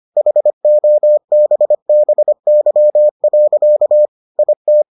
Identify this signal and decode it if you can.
Morse code, heard as a keyed tone, about 25 wpm, 600 Hz. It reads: HOBBY. IT